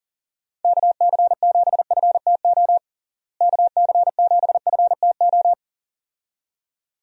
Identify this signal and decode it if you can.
Morse code, heard as a keyed tone, about 37 wpm, 700 Hz. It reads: KC7FTO KC7FTO
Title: USA Call Signs - 6 Characters - Repeated 2x - 40wpm